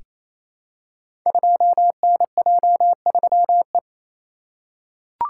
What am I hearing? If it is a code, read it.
Morse code, heard as a keyed tone, about 28 wpm, 690 Hz. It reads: 2NJ3E
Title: Code Groups - 5 Letter:Numbers - 28wpm